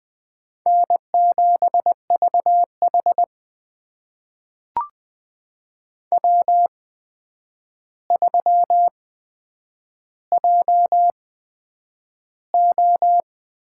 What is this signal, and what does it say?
Morse code, heard as a keyed tone, about 20 wpm, spaced as for 9 wpm, 700 Hz. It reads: N7VH W3JO
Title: USA Call Signs - 4 Characters - 8x Character Spacing - 20wpm